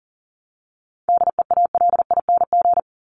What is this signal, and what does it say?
Morse code, heard as a keyed tone, about 40 wpm, 700 Hz. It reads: DEALING
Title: ING Suffix - Top 100 Words with Base Words 40wpm